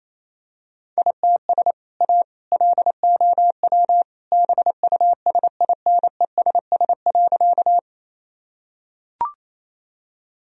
Morse code, heard as a keyed tone, about 28 wpm, 700 Hz. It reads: ITS A LOW BUSINESS.
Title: Sentences from Top 1000 Words - Review Part 1 - 28wpm